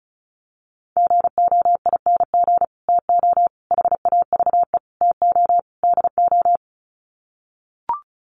Morse code, heard as a keyed tone, about 34 wpm, 700 Hz. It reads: GOING TO HAVE TO DO
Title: Sets of 5 Words Encore 35wpm